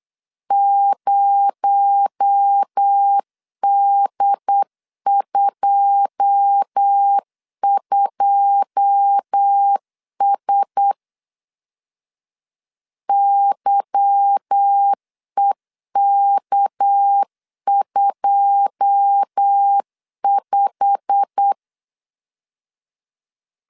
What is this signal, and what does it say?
Morse code, heard as a keyed tone, about 8 wpm, 800 Hz. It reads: 0D22S YEK25
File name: LEZIONI / 9 2 8wpm 3437